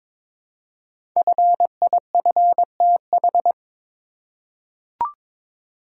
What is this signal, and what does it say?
Morse code, heard as a keyed tone, about 22 wpm, 700 Hz. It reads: FIFTH
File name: 5 Letter Words - First 3 Letters Given - 22wpm